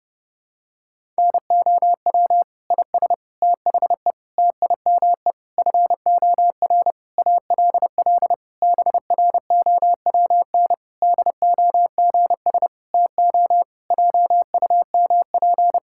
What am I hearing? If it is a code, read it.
Morse code, heard as a keyed tone, about 32 wpm, 700 Hz. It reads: NOW IS THE TIME FOR ALL BROWN DOGS TO JUMP
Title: Pangram Challenge v2 - 30wpm